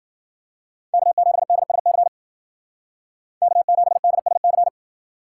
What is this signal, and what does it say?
Morse code, heard as a keyed tone, about 52 wpm, 700 Hz. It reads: K7DSC K7DSC